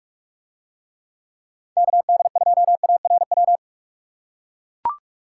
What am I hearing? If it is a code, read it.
Morse code, heard as a keyed tone, about 43 wpm, 700 Hz. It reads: KD2ARW